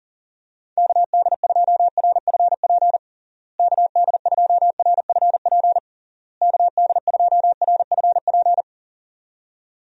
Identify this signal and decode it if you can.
Morse code, heard as a keyed tone, about 40 wpm, 700 Hz. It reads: KD2RFP KD2RFP KD2RFP
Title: USA Call Signs - 6 Characters - Repeated 3x - 40wpm